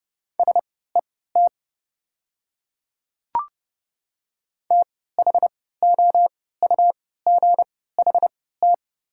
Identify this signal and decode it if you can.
Morse code, heard as a keyed tone, about 31 wpm, spaced as for 17 wpm, 700 Hz. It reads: SET THOUGHT
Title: 3x Character Spacing - Top 300 Words - 30wpm